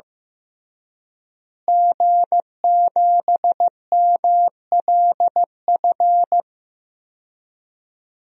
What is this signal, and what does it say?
Morse code, heard as a keyed tone, about 16 wpm, 700 Hz. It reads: G7MLF
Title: DXCC-100 Call Signs - 5 Characters - Repeated 3x - 15wpm